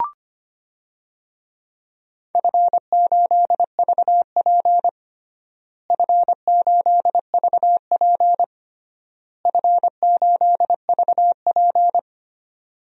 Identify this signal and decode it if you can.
Morse code, heard as a keyed tone, about 25 wpm, 700 Hz. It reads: F8VP F8VP F8VP
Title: DXCC-100 Call Signs - 4 Characters - Repeated 3x - 25wpm